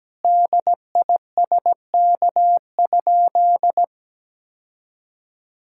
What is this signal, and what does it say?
Morse code, heard as a keyed tone, about 17 wpm, 700 Hz. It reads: DISK?
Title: Sentences from Top 1000 Words - Review Part 6 - 17wpm